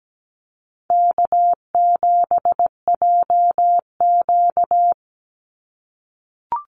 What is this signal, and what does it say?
Morse code, heard as a keyed tone, about 17 wpm, 700 Hz. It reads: K7JQ
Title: Easy Call Signs - 17wpm